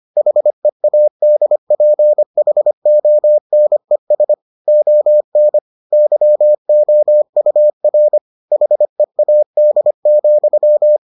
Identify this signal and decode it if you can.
Morse code, heard as a keyed tone, about 25 wpm, 600 Hz. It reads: HEADPHONES ON YOUR HEAD,